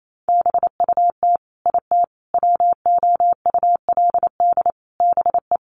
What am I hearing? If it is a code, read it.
Morse code, heard as a keyed tone, about 28 wpm, 700 Hz. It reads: BUT IT WOULD BE
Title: Sentences from Top 300 Words 28wpm